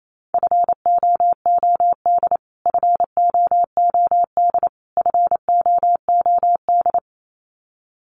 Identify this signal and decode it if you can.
Morse code, heard as a keyed tone, about 28 wpm, 700 Hz. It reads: FOOD FOOD FOOD
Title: Top 500 Words Repeated 28wpm